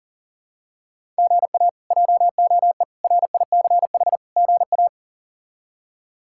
Morse code, heard as a keyed tone, about 40 wpm, 700 Hz. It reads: GA JOE RICH GA
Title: K1USN SST QSO Practice - 40wpm